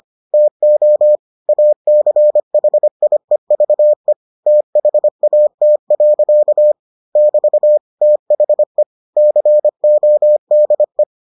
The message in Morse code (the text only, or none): TO ACHIEVE THAT. <BT> THE CODE